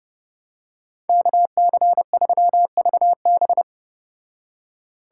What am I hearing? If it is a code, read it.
Morse code, heard as a keyed tone, about 29 wpm, 700 Hz. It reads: KC3VB